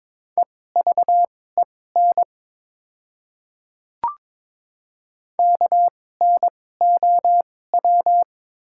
Morse code, heard as a keyed tone, about 22 wpm, spaced as for 16 wpm, 700 Hz. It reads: EVEN KNOW
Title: 2x Character Spacing - Top 300 Words - 22wpm